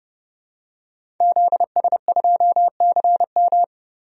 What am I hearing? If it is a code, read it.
Morse code, heard as a keyed tone, about 30 wpm, 700 Hz. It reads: ZS2CM
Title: DXCC-100 Call Signs - 5 Characters - Repeated 2x - 30wpm